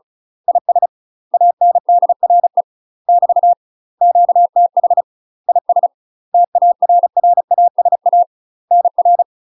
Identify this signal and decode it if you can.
Morse code, heard as a keyed tone, about 35 wpm, 700 Hz. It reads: IS ANDRE <BT> QTH IS TARRASA NR
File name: UK MASTER QSOs TRAINING / 35WPM / UK MASTER QSOs 0005